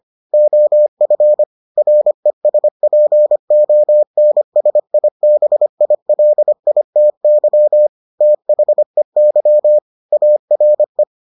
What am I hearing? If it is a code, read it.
Morse code, heard as a keyed tone, about 25 wpm, 600 Hz. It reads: OF RESPONSIBILITY THEY ARE